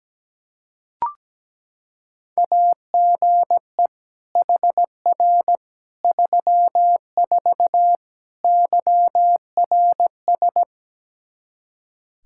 Morse code, heard as a keyed tone, about 17 wpm, 700 Hz. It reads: AGE HR 34 YRS
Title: Practice QSO - Exchange 3 - More Info - 17wpm